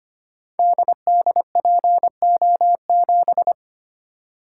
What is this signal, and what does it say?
Morse code, heard as a keyed tone, about 25 wpm, 700 Hz. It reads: DDPO7
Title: Code Groups - 5 Letter:Numbers - 25wpm